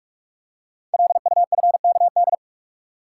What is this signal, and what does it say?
Morse code, heard as a keyed tone, about 44 wpm, 700 Hz. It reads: LUFKD
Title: Code Groups - 5 Letters - 45wpm